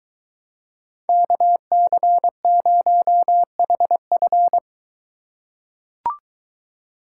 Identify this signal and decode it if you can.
Morse code, heard as a keyed tone, about 21 wpm, 700 Hz. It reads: KC0HF